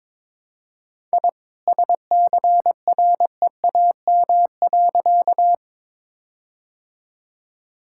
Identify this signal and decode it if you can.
Morse code, heard as a keyed tone, about 22 wpm, 700 Hz. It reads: I SCREAM.
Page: morse-code-ninja-practice-22wpm